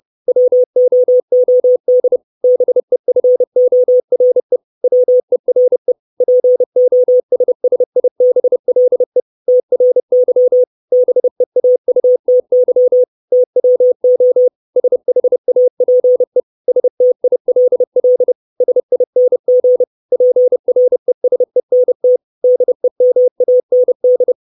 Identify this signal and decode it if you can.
Morse code, heard as a keyed tone, about 30 wpm, 500 Hz. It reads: WOOD BEFORE WERE POSSIBLE TRY BEAUTY TWO SHAPE STILL SING PRESENT DEMAND
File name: 632 Words 30WPM